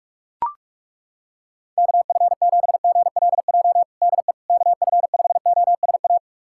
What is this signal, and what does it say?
Morse code, heard as a keyed tone, about 42 wpm, 690 Hz. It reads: KF7GLJ DE KF5OSA
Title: USA Callsign DE Callsign - 6 Characters - 45wpm